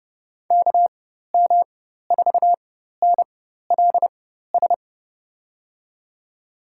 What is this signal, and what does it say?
Morse code, heard as a keyed tone, about 30 wpm, spaced as for 14 wpm, 700 Hz. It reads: KM4NLS
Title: USA Call Signs - 6 Characters - 4x Character Spacing - 30wpm